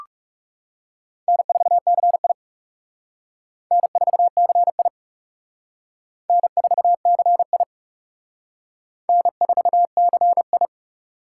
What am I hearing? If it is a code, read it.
Morse code, heard as a keyed tone, about 37 wpm, 700 Hz. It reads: N4CI N4CI N4CI N4CI
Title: Speed Racing - USA 1x2 Call Signs 30wpm